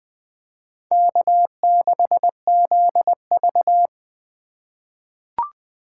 Morse code, heard as a keyed tone, about 20 wpm, 700 Hz. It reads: K6ZV
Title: USA Call Signs - 4 Characters - 3x Character Spacing - 20wpm